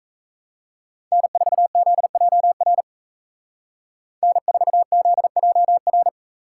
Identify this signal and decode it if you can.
Morse code, heard as a keyed tone, about 33 wpm, 700 Hz. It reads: N4ZJR N4ZJR
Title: Speed Racing - USA 1x3 Call Signs 28wpm